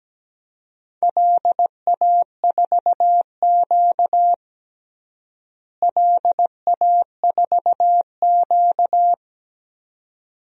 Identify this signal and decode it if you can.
Morse code, heard as a keyed tone, about 17 wpm, 700 Hz. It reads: LA4Q LA4Q
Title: DXCC-100 Call Signs - 4 Characters - Repeated 2x - 17wpm